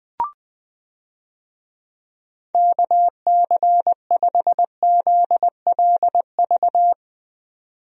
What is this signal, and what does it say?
Morse code, heard as a keyed tone, about 21 wpm, 700 Hz. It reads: KC5ZLV
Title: USA Call Signs - 6 Characters - 20wpm